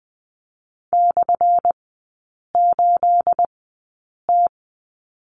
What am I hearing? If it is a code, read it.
Morse code, heard as a keyed tone, about 20 wpm, spaced as for 8 wpm, 700 Hz. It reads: /8T
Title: 212 Lesson 34e - Number 8 - Three Characters 20wpm